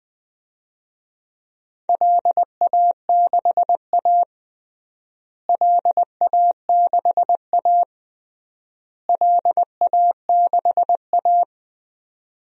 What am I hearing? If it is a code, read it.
Morse code, heard as a keyed tone, about 20 wpm, 700 Hz. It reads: LA6A LA6A LA6A
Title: DXCC-100 Call Signs - 4 Characters - Repeated 3x - 20wpm